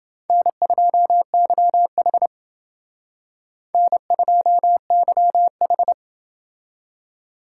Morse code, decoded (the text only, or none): N2YH N2YH